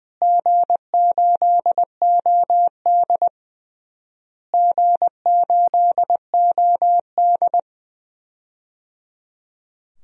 Morse code, heard as a keyed tone, about 20 wpm, 700 Hz. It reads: G8OD G8OD
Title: DXCC-100 Call Signs - 4 Characters - Repeated 3x - 20wpm